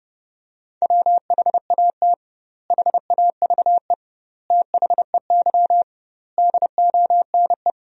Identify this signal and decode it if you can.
Morse code, heard as a keyed tone, about 30 wpm, 700 Hz. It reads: WHAT HAVE THEY DONE
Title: 2x Word-Spacing - Sentences from Top 500 Words - 30wpm